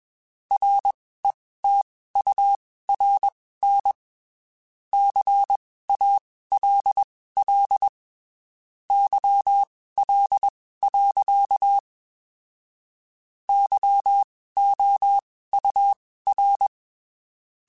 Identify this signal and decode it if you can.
Morse code, heard as a keyed tone, about 21 wpm, spaced as for 15 wpm, 800 Hz. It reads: RETURN CALL YL. YOUR
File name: (spaced) / QSOs Spaced-mp3s / WA7QOX 2 @20 sp